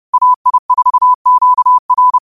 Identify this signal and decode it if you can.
Morse code, heard as a keyed tone, about 30 wpm, 1000 Hz. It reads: AI4QR